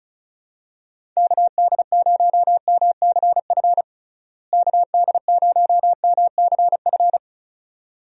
Morse code, heard as a keyed tone, about 36 wpm, 700 Hz. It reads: KD0MCF KD0MCF
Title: USA Call Signs - 6 Characters - Repeated 2x - 35wpm